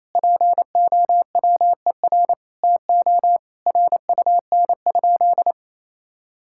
Morse code, heard as a keyed tone, about 28 wpm, 700 Hz. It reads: POWER TO RUN?
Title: Sentences from Top 1000 Words - Review Part 6 - 28wpm